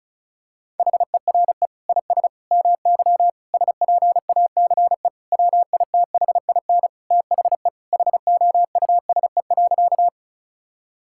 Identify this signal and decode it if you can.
Morse code, heard as a keyed tone, about 36 wpm, 700 Hz. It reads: HERE IS MY SPACE WITHIN THE HOUSE.
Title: Sentences from Top 1000 Words - Review Part 5 - 35wpm